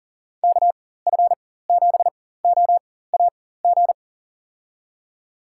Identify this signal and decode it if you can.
Morse code, heard as a keyed tone, about 40 wpm, spaced as for 19 wpm, 700 Hz. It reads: KF7OAG
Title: USA Call Signs - 6 Characters - 4x Character Spacing - 40wpm